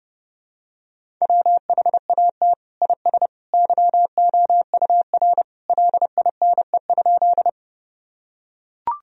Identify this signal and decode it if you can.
Morse code, heard as a keyed tone, about 30 wpm, 700 Hz. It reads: WHAT IS YOUR LINE?
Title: Sentences from Top 1000 Words - Review Part 1 - 30wpm